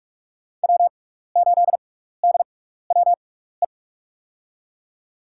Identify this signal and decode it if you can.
Morse code, heard as a keyed tone, about 46 wpm, spaced as for 16 wpm, 690 Hz. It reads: W8DWE
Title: USA Call Signs - 5 Characters - 6x Character Spacing - 45wpm